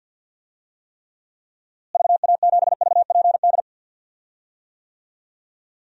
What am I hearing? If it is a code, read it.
Morse code, heard as a keyed tone, about 52 wpm, 700 Hz. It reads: VA7VPD
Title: DXCC-100 Call Signs - 6 Characters - Repeated 3x - 50wpm